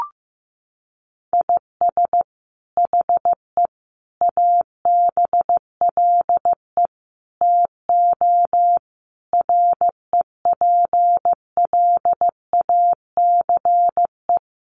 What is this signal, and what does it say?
Morse code, heard as a keyed tone, about 15 wpm, 700 Hz. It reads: IS HE ABLE TO REPLACE
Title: Sentences from Top 1400 Words 15wpm